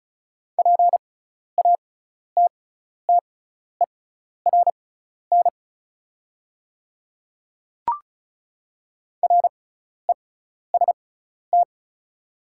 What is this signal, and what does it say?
Morse code, heard as a keyed tone, about 35 wpm, spaced as for 12 wpm, 690 Hz. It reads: PATTERN REST
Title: 6x Character Spacing - Top 500 Words - 35wpm